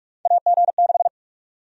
Morse code, heard as a keyed tone, about 46 wpm, 700 Hz. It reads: AG6
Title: USA Call Sign Prefixes - Repeated 2x - 45wpm